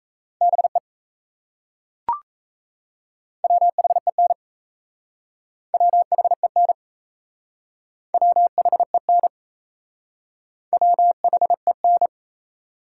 Morse code, heard as a keyed tone, about 36 wpm, 700 Hz. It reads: BE WHEN WHEN WHEN WHEN